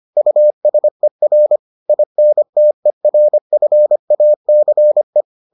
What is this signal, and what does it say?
Morse code, heard as a keyed tone, about 25 wpm, 600 Hz. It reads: USER INTERFACE